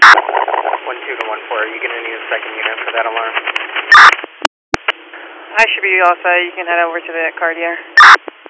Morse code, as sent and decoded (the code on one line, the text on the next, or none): none